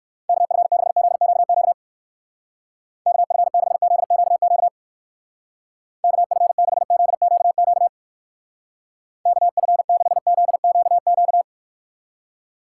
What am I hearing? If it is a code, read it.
Morse code, heard as a keyed tone, about 57 wpm, 700 Hz. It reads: KF6ZQQ KF6ZQQ KF6ZQQ KF6ZQQ